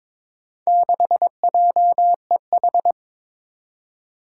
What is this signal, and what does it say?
Morse code, heard as a keyed tone, about 22 wpm, 700 Hz. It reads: 6JEH
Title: Four Challenging Letter-Number Combinations 22wpm